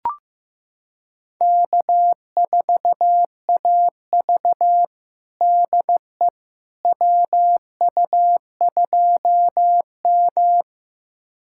Morse code, heard as a keyed tone, about 15 wpm, 700 Hz. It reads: K4AV DE WU2M
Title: USA Callsign DE Callsign - 4 Characters - 15wpm